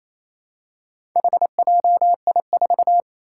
Morse code, heard as a keyed tone, about 28 wpm, 700 Hz. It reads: HJI4
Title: Four Challenging Letter-Number Combinations 28wpm